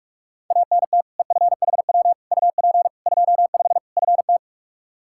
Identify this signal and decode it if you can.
Morse code, heard as a keyed tone, about 45 wpm, 700 Hz. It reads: ANT EFHW UP 25 FT
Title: Practice QSO - Exchange 2 - Info - 45wpm